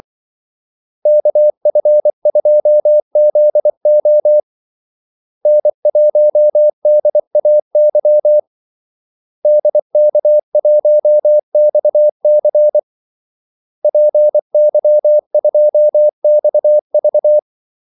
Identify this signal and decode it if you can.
Morse code, heard as a keyed tone, about 24 wpm, 600 Hz. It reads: KF2ZO N1DAY DK1XC PY2XV
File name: Callsigns-24wpm